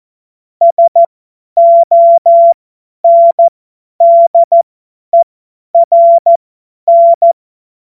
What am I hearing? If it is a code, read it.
Morse code, heard as a keyed tone, about 14 wpm, spaced as for 10 wpm, 680 Hz. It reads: SONDERN
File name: Witz-16-September-2025-50-F70